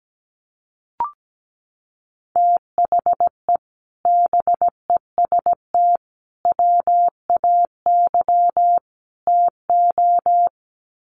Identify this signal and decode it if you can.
Morse code, heard as a keyed tone, about 17 wpm, 700 Hz. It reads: THE BEST WAY TO
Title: Sets of 4 Words 17wpm